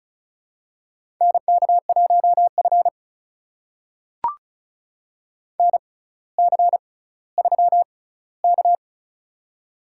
Morse code, heard as a keyed tone, about 36 wpm, spaced as for 18 wpm, 700 Hz. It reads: NK1F NC3K
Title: USA Call Signs - 4 Characters - 6x Character Spacing - 35wpm